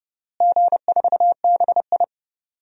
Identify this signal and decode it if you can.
Morse code, heard as a keyed tone, about 30 wpm, 700 Hz. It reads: G4BI
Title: DXCC-100 Call Signs - 4 Characters - Repeated 2x - 30wpm